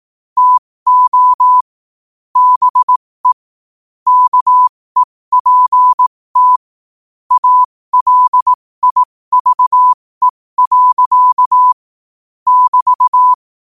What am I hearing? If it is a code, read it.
Morse code, heard as a keyed tone, about 18 wpm, spaced as for 15 wpm, 1000 Hz. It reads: TO BE KEPT ALIVE. <BT>
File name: qotdfiles / 15wpm